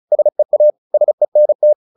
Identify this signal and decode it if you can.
Morse code, heard as a keyed tone, about 34 wpm, 600 Hz. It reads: SEA SENT
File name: the old man and the sea / the old man and the sea 1 35wpm